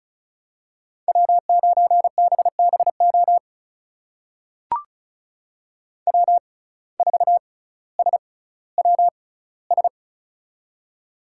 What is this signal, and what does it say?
Morse code, heard as a keyed tone, about 34 wpm, spaced as for 18 wpm, 700 Hz. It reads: W9BBO W4SWS
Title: USA Call Signs - 5 Characters - 6x Character Spacing - 35wpm